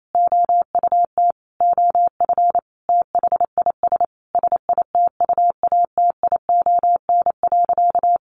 Morse code, heard as a keyed tone, about 28 wpm, 700 Hz. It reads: OUT OF THIS SITUATION.
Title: Sentences from Top 1200 Words 28wpm